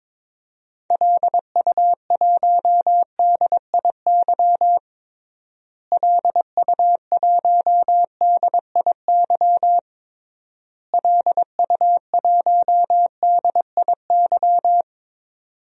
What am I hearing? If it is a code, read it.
Morse code, heard as a keyed tone, about 22 wpm, 700 Hz. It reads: LU1DIY LU1DIY LU1DIY
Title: DXCC-100 Call Signs - 6 Characters - Repeated 3x - 22wpm